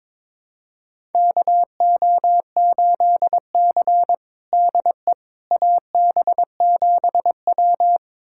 Morse code, heard as a keyed tone, about 22 wpm, 700 Hz. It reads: KO8C DE AB7W
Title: USA Callsign DE Callsign - 4 Characters - 22wpm